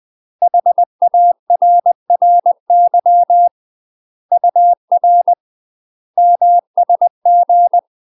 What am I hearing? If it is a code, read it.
Morse code, heard as a keyed tone, about 20 wpm, 700 Hz. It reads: HARRY UR MSG
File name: UK MASTER QSOs TRAINING / 20WPM / UK MASTER QSOs 0051